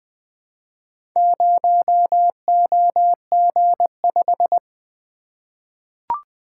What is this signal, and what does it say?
Morse code, heard as a keyed tone, about 20 wpm, 700 Hz. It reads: 0OG5
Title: Four Letter-Number Combinations - 20wpm